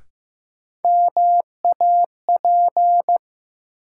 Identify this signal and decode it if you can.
Morse code, heard as a keyed tone, about 15 wpm, 700 Hz. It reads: MAP
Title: Sentences from Top 400 Words - New Word Drill - 15wpm